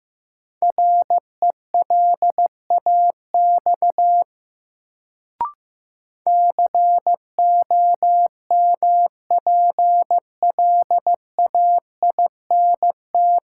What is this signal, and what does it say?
Morse code, heard as a keyed tone, about 16 wpm, 700 Hz. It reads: RELAX COMPLAINT
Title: Sentences from Top 1400 Words - New Word Drill - 15wpm